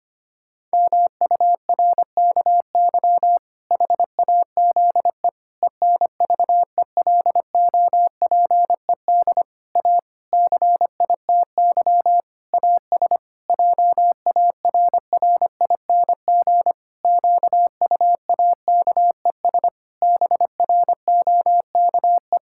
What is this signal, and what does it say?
Morse code, heard as a keyed tone, about 25 wpm, 700 Hz. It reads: MURKY HAZE ENVELOPED A CITY AS JARRING QUAKES BROKE
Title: Pangram Challenge v2 - 25wpm